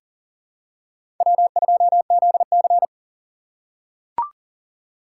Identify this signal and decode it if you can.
Morse code, heard as a keyed tone, about 38 wpm, 700 Hz. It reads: W2ZC
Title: USA Call Signs - 3x Character Spacing - 40wpm